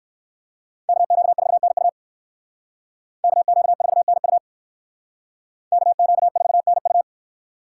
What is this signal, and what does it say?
Morse code, heard as a keyed tone, about 63 wpm, 700 Hz. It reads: KQ4NU KQ4NU KQ4NU